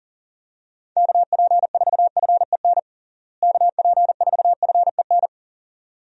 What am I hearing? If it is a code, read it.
Morse code, heard as a keyed tone, about 40 wpm, 700 Hz. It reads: KP4FEN KP4FEN
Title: DXCC-100 Call Signs - 6 Characters - Repeated 2x - 40wpm